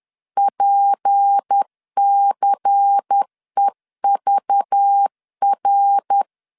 Morse code, heard as a keyed tone, about 10 wpm, 800 Hz. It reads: PCEVR
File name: CW / DET / 1 9 1500C S B 10wpm 4439